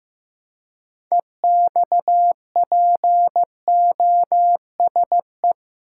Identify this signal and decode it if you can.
Morse code, heard as a keyed tone, about 15 wpm, 700 Hz. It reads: EXPOSE
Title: Six Letter Words - 15wpm